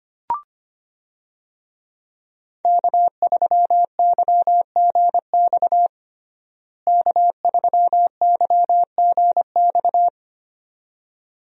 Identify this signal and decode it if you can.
Morse code, heard as a keyed tone, about 25 wpm, 700 Hz. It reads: K3YGX K3YGX
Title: USA Call Signs - 5 Characters - Repeated 2x - 25wpm